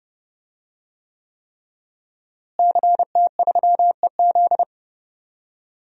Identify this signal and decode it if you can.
Morse code, heard as a keyed tone, about 32 wpm, 700 Hz. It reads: CT3EZ
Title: DXCC-100 Call Signs - 5 Characters - Repeated 3x - 30wpm